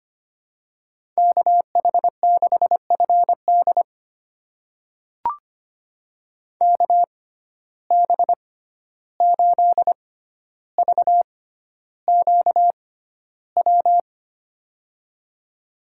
Morse code, heard as a keyed tone, about 25 wpm, spaced as for 12 wpm, 700 Hz. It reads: KH6FD KB8VQW
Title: USA Call Signs - 6x Character Spacing - 25wpm